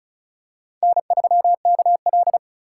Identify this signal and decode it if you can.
Morse code, heard as a keyed tone, about 36 wpm, 700 Hz. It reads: N3KL
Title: USA Call Signs - 4 Characters - 4x Character Spacing - 35wpm